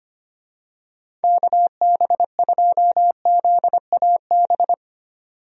Morse code, heard as a keyed tone, about 25 wpm, 700 Hz. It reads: KB2ZAB
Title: USA Call Signs - 6 Characters - 25wpm